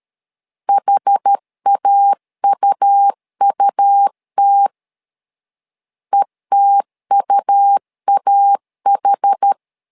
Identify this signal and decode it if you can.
Morse code, heard as a keyed tone, about 13 wpm, 800 Hz. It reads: HAUUT ETUAH
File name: LEZIONI / 2 UHDN / 2 UH 12wpm 1439